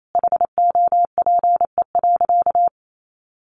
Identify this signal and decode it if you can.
Morse code, heard as a keyed tone, about 28 wpm, 700 Hz. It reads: HOPE.
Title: Sentences from Top 1100 Words 28wpm